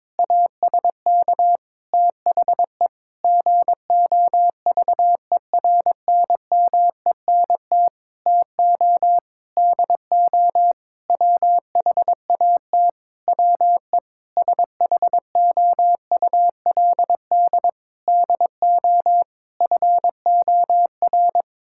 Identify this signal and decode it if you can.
Morse code, heard as a keyed tone, about 22 wpm, 700 Hz. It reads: ASK THE GOVERNMENT TO DO WHAT WE SHOULD DO FOR
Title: Sentences from Top 1600 Words 22wpm